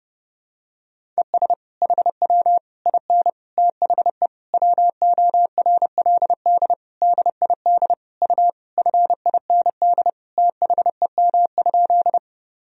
Morse code, heard as a keyed tone, about 30 wpm, 700 Hz. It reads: ES HW IN THE WORLD DID U FIND THEM?
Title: Sentences from Top 200 Words and Abbreviations 30wpm